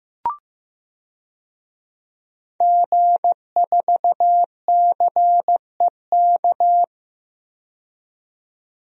Morse code, heard as a keyed tone, about 15 wpm, 700 Hz. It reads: G4CEK
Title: DXCC-100 Call Signs - 5 Characters - 15wpm